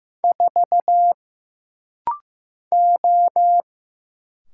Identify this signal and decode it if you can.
Morse code, heard as a keyed tone, about 15 wpm, 700 Hz. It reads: 4 O
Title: Single Letter-Number Alt - Rapid-Fire 15wpm